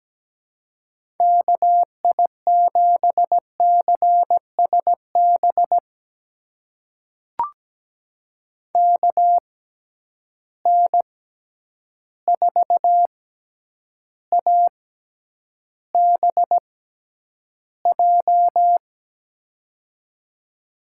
Morse code, heard as a keyed tone, about 17 wpm, spaced as for 9 wpm, 700 Hz. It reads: KI7CSB KN4ABJ
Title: USA Call Signs - 6 Characters - 6x Character Spacing - 17wpm